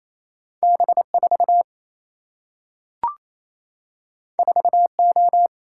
Morse code, heard as a keyed tone, about 28 wpm, 700 Hz. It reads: B 4 4 O
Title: Single-Letter Single-Number 28wpm